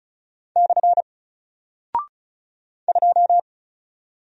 Morse code, heard as a keyed tone, about 35 wpm, 700 Hz. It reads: / 2
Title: Single Character (Letter:Number:Punctuation:Prosign) 35wpm